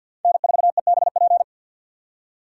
Morse code, heard as a keyed tone, about 47 wpm, 700 Hz. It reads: N4EBP
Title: USA Call Signs - 5 Characters - 3x Character Spacing - 50wpm